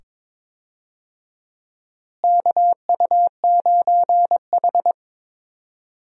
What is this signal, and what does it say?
Morse code, heard as a keyed tone, about 22 wpm, 700 Hz. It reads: KU9H
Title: USA Call Signs - 4 Characters - Repeated 2x - 22wpm